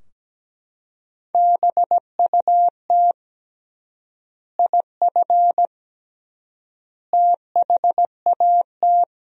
Morse code, heard as a keyed tone, about 17 wpm, 700 Hz. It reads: BUT IF THAT
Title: 3x Word-Spacing - Sentences from Top 200 Words - 17wpm